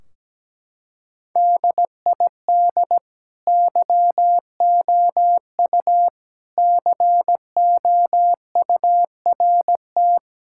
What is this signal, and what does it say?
Morse code, heard as a keyed tone, about 17 wpm, 700 Hz. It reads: DID YOU COURT
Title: Sentences from Top 1600 Words 17wpm